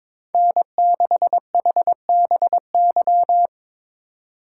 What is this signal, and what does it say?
Morse code, heard as a keyed tone, about 22 wpm, 700 Hz. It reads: N6HBY
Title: USA Call Signs - 5 Characters - 4x Character Spacing - 22wpm